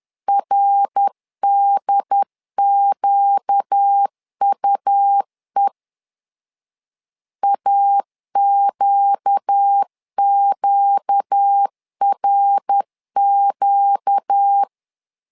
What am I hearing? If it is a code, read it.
Morse code, heard as a keyed tone, about 10 wpm, 800 Hz. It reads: RDQUE AQQRQ
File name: LEZIONI / 4 Q 10wpm 2210